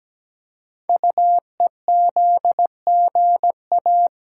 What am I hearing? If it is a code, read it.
Morse code, heard as a keyed tone, about 17 wpm, 700 Hz. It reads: UEZGA
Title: Code Groups - 5 Letter:Numbers - 17wpm